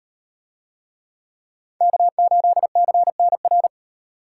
Morse code, heard as a keyed tone, about 38 wpm, 700 Hz. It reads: K8CNR